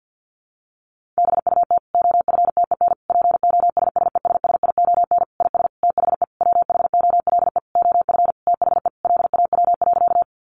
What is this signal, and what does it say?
Morse code, heard as a keyed tone, about 50 wpm, 700 Hz. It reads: BUT OFTEN POSSESSION IS THE WHOLE OF THE LAW.
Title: Sentences from Top 1400 Words 50wpm